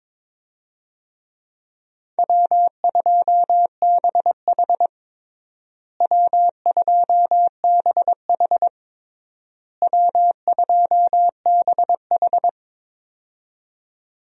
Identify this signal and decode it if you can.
Morse code, heard as a keyed tone, about 22 wpm, 700 Hz. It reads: W2BH W2BH W2BH
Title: USA Call Signs - 4 Characters - Repeated 3x - 22wpm